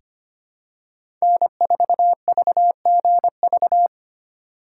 Morse code, heard as a keyed tone, about 25 wpm, 700 Hz. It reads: N4VGV
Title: USA Call Signs - 4x Character Spacing - 25wpm